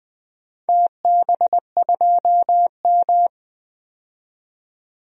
Morse code, heard as a keyed tone, about 20 wpm, 700 Hz. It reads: TB2M
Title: DXCC-100 Call Signs - 4 Characters - 20wpm